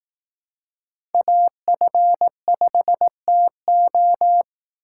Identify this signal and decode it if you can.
Morse code, heard as a keyed tone, about 18 wpm, 700 Hz. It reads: AF5TO
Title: Speed Racing - USA 2x2 Call Signs 15wpm